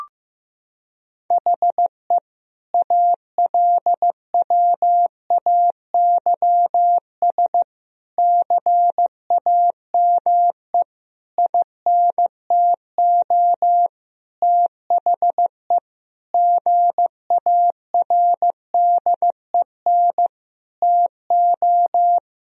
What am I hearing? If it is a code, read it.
Morse code, heard as a keyed tone, about 15 wpm, 700 Hz. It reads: HE ALWAYS CAME INTO THE GARDEN TO
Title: Sentences from Top 1000 Words - Review Part 4 - 15wpm